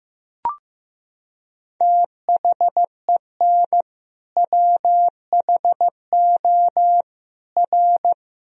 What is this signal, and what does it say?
Morse code, heard as a keyed tone, about 16 wpm, 700 Hz. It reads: THEN WHO R
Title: Sentences from Top 100 Words and Abbreviations 15wpm